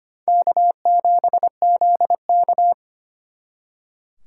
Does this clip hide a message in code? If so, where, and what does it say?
Yes: Morse code from 0.3 to 2.7 s, heard as a keyed tone, about 32 wpm, 700 Hz. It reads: K7ZK